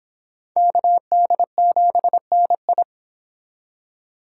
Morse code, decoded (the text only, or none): KD7NI